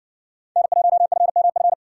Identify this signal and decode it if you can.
Morse code, heard as a keyed tone, about 60 wpm, 700 Hz. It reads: N1FMF